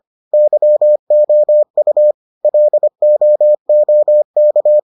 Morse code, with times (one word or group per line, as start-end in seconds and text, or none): 0.3-2.1 YOU
2.4-4.8 LOOK